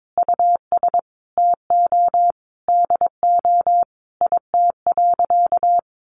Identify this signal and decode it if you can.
Morse code, heard as a keyed tone, about 22 wpm, 700 Hz. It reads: US TO DO IT.
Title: Sentences from Top 1500 Words 22wpm